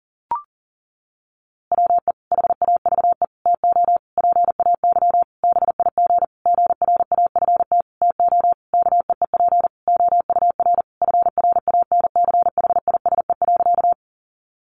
Morse code, heard as a keyed tone, about 37 wpm, 700 Hz. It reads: WE HAVE TO PAY BIG GRAFT TO KEEP OUR FRANCHISE.
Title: A Sound Of Thunder 40wpm